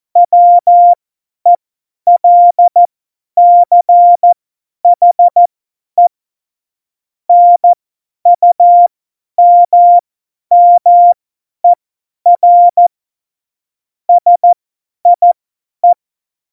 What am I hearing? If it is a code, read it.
Morse code, heard as a keyed tone, about 14 wpm, spaced as for 10 wpm, 700 Hz. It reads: WELCHE NUMMER SIE
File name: Morsekurs / Witze / Witz 11 August 2020, 50 F70